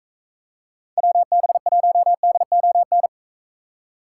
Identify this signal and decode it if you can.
Morse code, heard as a keyed tone, about 42 wpm, 700 Hz. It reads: WB1DON